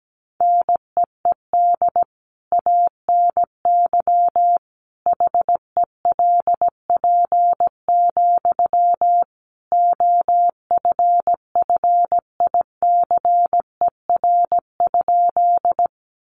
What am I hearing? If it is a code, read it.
Morse code, heard as a keyed tone, about 17 wpm, 700 Hz. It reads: NEED ANY HELP, OFFICER?
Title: Sentences from Top 1200 Words 17wpm